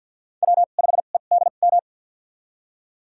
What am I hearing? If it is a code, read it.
Morse code, heard as a keyed tone, about 52 wpm, spaced as for 36 wpm, 700 Hz. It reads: W5EDM
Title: USA Call Signs - 5 Characters - 2x Character Spacing - 50wpm